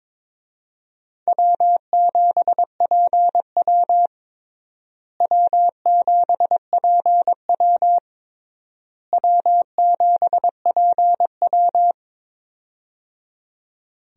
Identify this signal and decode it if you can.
Morse code, heard as a keyed tone, about 21 wpm, 700 Hz. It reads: W7PW W7PW W7PW
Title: USA Call Signs - 4 Characters - Repeated 3x - 22wpm